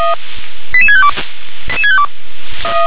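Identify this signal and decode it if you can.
Five-tone selective call (ZVEI), heard as quick runs of five tones, tones of about 70 ms. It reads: 88652
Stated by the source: ZVEI five-tone selective call received on an SDR.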